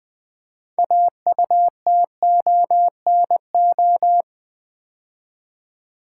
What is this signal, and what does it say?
Morse code, heard as a keyed tone, about 20 wpm, 700 Hz. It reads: AUTONO
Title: Build A Word - 3 Letters At A Time - 8-Letter Words - 20wpm